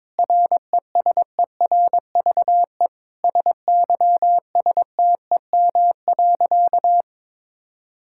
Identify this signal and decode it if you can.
Morse code, heard as a keyed tone, about 22 wpm, 700 Hz. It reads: RESERVE SYSTEM.
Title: Sentences from Top 1000 Words - Review Part 6 - 22wpm